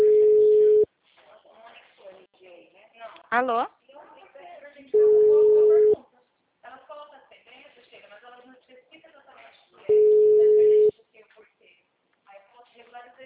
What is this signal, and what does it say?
Ringback tone (Europe, 425 Hz), with speech.